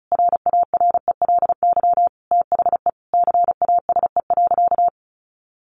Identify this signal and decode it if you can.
Morse code, heard as a keyed tone, about 34 wpm, 700 Hz. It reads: RARELY THE CASE.
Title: Sentences from Top 1400 Words 35wpm